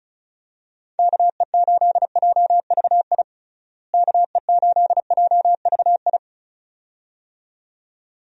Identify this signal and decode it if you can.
Morse code, heard as a keyed tone, about 36 wpm, 700 Hz. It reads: KE8JVI KE8JVI
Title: USA Call Signs - 6 Characters - Repeated 2x - 35wpm